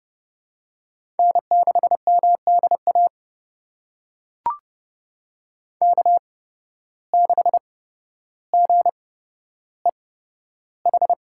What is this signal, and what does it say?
Morse code, heard as a keyed tone, about 31 wpm, spaced as for 13 wpm, 700 Hz. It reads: N6MDA K6GEH
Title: USA Call Signs - 5 Characters - 8x Character Spacing - 30wpm